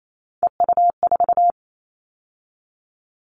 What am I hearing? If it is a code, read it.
Morse code, heard as a keyed tone, about 28 wpm, 700 Hz. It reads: EU4
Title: Callsign Prefixes for Top 100 DXCC - 28wpm